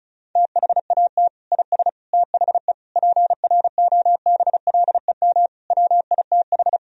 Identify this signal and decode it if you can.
Morse code, heard as a keyed tone, about 36 wpm, 700 Hz. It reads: THAT IS THE PROBLEM WITH
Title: Sentences from Top 1000 Words - Review Part 2 - 35wpm